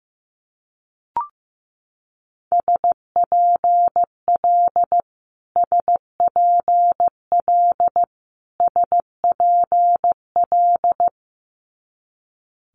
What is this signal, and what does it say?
Morse code, heard as a keyed tone, about 15 wpm, 700 Hz. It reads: SPL SPL SPL
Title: Multi-letter Phonemes Repeated 3x 15wpm